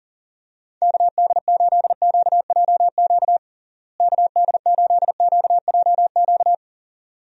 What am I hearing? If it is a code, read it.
Morse code, heard as a keyed tone, about 37 wpm, 700 Hz. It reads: KD8QJQ KD8QJQ
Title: USA Call Signs - 6 Characters - Repeated 2x - 40wpm